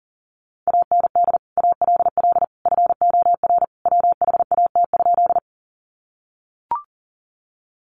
Morse code, heard as a keyed tone, about 40 wpm, 700 Hz. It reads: AND ALL FOR WHAT?
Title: Sentences from Top 100 Words 40wpm